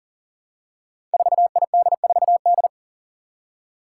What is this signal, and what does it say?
Morse code, heard as a keyed tone, about 40 wpm, 700 Hz. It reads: 4ID4D
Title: Five Challenging Letter-Number Combinations 40wpm